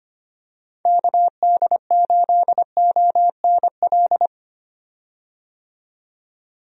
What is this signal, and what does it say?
Morse code, heard as a keyed tone, about 25 wpm, 700 Hz. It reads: KD8ONL
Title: USA Call Signs - 6 Characters - 25wpm